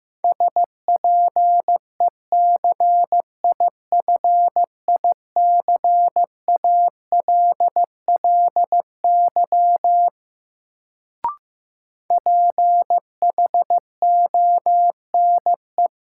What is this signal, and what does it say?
Morse code, heard as a keyed tone, about 15 wpm, 700 Hz. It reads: SPECIFICALLY PHONE
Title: Sentences from Top 1100 Words - New Word Drill - 15wpm